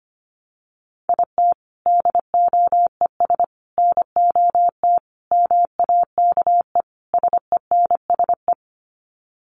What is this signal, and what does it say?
Morse code, heard as a keyed tone, about 25 wpm, 700 Hz. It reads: IT DOES NOT MAKE SENSE
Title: Sets of 5 Words Encore2 25wpm